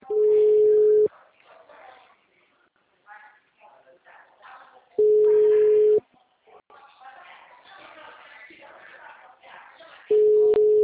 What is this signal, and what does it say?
Ringback tone (Europe, 425 Hz), with speech.
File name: out- -2003- - - .16696